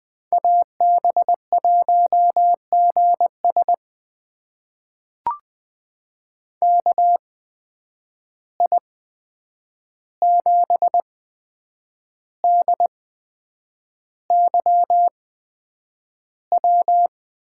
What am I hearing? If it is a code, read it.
Morse code, heard as a keyed tone, about 20 wpm, spaced as for 8 wpm, 700 Hz. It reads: AB1GS KI7DYW
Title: USA Call Signs - 8x Character Spacing - 20wpm